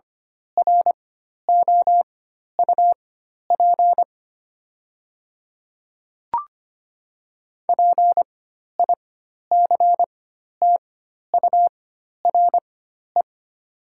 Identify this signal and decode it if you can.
Morse code, heard as a keyed tone, about 25 wpm, spaced as for 12 wpm, 700 Hz. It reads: ROUP PICTURE
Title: 4x Character Spacing - Top 300 Words - 25wpm